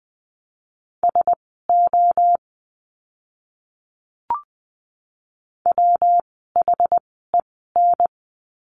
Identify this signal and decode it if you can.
Morse code, heard as a keyed tone, about 20 wpm, spaced as for 14 wpm, 700 Hz. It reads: SO WHEN
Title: 139 Lesson 23f - Letter f - Words 20wpm